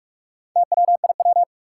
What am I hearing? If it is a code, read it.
Morse code, heard as a keyed tone, about 44 wpm, 700 Hz. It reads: TWIW
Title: Four Challenging Letter-Number Combinations 45wpm